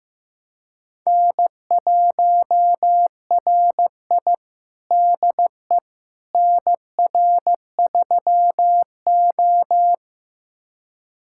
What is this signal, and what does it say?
Morse code, heard as a keyed tone, about 15 wpm, 700 Hz. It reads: N1RI DE NR3O
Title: USA Callsign DE Callsign - 4 Characters - 15wpm